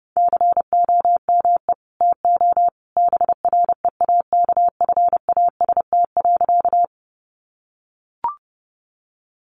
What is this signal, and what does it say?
Morse code, heard as a keyed tone, about 30 wpm, 700 Hz. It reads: COME TO BREAKFAST.
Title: Sentences from Top 1500 Words 30wpm